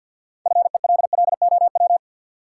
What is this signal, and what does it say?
Morse code, heard as a keyed tone, about 52 wpm, 700 Hz. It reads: FELLOW
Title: Sets of 5 Words Encore - New Word Drill - 50wpm